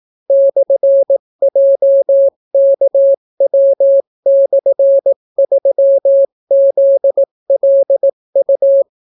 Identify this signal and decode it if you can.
Morse code, heard as a keyed tone, about 18 wpm, 550 Hz. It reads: /JKW/3ZLU